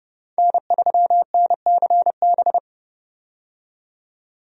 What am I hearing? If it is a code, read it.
Morse code, heard as a keyed tone, about 30 wpm, 700 Hz. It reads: N3NCB
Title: USA Call Signs - 5 Characters - 30wpm